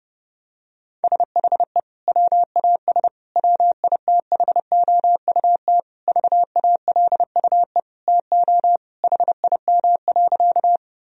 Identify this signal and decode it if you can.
Morse code, heard as a keyed tone, about 30 wpm, 700 Hz. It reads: SHE WAS WITHOUT VALUE TO HIM.
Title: Sentences from Top 1000 Words - Review Part 3 - 30wpm